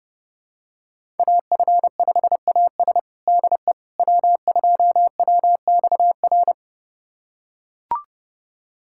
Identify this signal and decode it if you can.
Morse code, heard as a keyed tone, about 30 wpm, 700 Hz. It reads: AF5AS DE W2WXR
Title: USA Callsign DE Callsign - 5 Characters - 30wpm